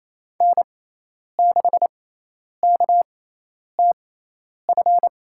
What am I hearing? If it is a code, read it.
Morse code, heard as a keyed tone, about 28 wpm, spaced as for 10 wpm, 700 Hz. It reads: N6KTF
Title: USA Call Signs - 6x Character Spacing - 28wpm